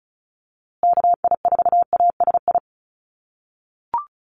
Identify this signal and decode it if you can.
Morse code, heard as a keyed tone, about 34 wpm, 700 Hz. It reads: KI4ASI
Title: USA Call Signs Repeated 3x 35wpm